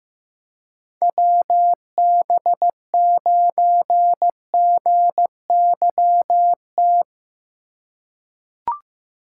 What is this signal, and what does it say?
Morse code, heard as a keyed tone, about 15 wpm, 700 Hz. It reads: WB9GYT
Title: USA Call Signs - 6 Characters - 6x Character Spacing - 15wpm